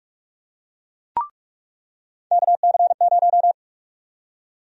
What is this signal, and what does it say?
Morse code, heard as a keyed tone, about 45 wpm, 700 Hz. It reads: KC0
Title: USA Call Sign Prefixes - 45wpm